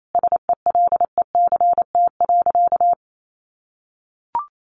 Morse code, heard as a keyed tone, about 28 wpm, 700 Hz. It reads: SELECT.
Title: Sentences from Top 1300 Words 28wpm